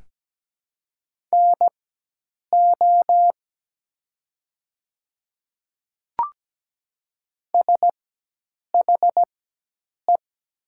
Morse code, heard as a keyed tone, about 17 wpm, spaced as for 8 wpm, 700 Hz. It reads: NO SHE
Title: 4x Character Spacing - Top 100 Words - 17wpm